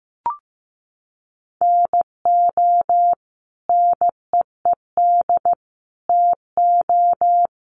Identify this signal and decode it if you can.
Morse code, heard as a keyed tone, about 15 wpm, 700 Hz. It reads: NO NEED TO
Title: Sentences from Top 1100 Words 15wpm